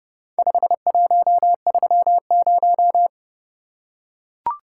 Morse code, heard as a keyed tone, about 30 wpm, 700 Hz. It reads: 5130